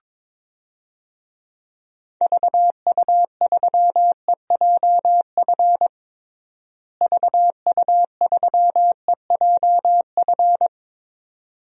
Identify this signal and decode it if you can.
Morse code, heard as a keyed tone, about 22 wpm, 700 Hz. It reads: VU3EJF VU3EJF
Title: DXCC-100 Call Signs - 6 Characters - Repeated 2x - 22wpm